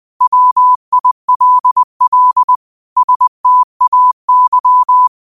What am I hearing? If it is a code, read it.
Morse code, heard as a keyed tone, about 20 wpm, 1000 Hz. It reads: WILL STAY
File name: qotdfiles / 20wpm